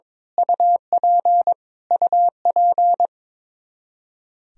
Morse code, heard as a keyed tone, about 22 wpm, 700 Hz. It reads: UP UP
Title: Top 100 Words Repeated 22wpm